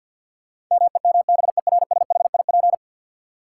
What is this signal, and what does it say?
Morse code, heard as a keyed tone, about 52 wpm, 700 Hz. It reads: MEMBERSHIP
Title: Sentences from Top 1400 Words - New Word Drill - 50wpm